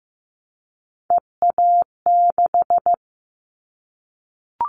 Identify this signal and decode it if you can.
Morse code, heard as a keyed tone, about 15 wpm, 700 Hz. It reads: EA6
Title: Callsign Prefixes for Top 100 DXCC - 15wpm